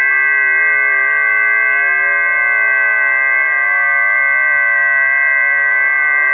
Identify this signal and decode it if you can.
FT8.
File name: 008-FT8-7047khz